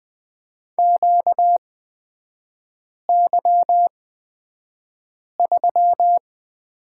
Morse code, heard as a keyed tone, about 20 wpm, spaced as for 5 wpm, 700 Hz. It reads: QY3
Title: 236 Lesson 38e - Letter q - Three Characters 20wpm